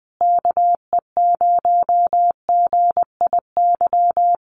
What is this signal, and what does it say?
Morse code, heard as a keyed tone, about 20 wpm, 700 Hz. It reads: KE0GIY
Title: Common QSO Phrases Encore 20wpm